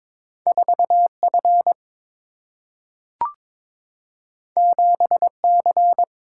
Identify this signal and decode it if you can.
Morse code, heard as a keyed tone, about 22 wpm, 700 Hz. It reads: 4F 7C
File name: practice / 22wpm / Two Letter-Number Combinations - 22wpm